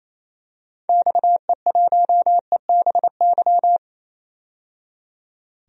Morse code, heard as a keyed tone, about 28 wpm, 700 Hz. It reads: XE1EBY
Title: DXCC-100 Call Signs - 6 Characters - 28wpm